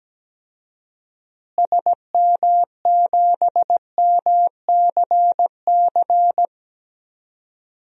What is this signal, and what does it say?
Morse code, heard as a keyed tone, about 17 wpm, 700 Hz. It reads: SM7MCC
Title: DXCC-100 Call Signs - 6 Characters - Repeated 2x - 17wpm